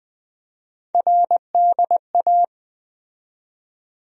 Morse code, heard as a keyed tone, about 20 wpm, 700 Hz. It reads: RDA
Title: USA Call Sign Postfixes - 3 Letters 20wpm